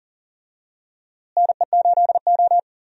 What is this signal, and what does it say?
Morse code, heard as a keyed tone, about 40 wpm, 700 Hz. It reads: NE8O
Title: USA Call Signs - 4 Characters - Repeated 2x - 40wpm